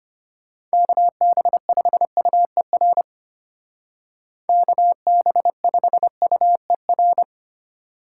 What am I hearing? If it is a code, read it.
Morse code, heard as a keyed tone, about 30 wpm, 700 Hz. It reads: KB5UER KB5UER
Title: Speed Racing - USA 2x3 Call Signs 25wpm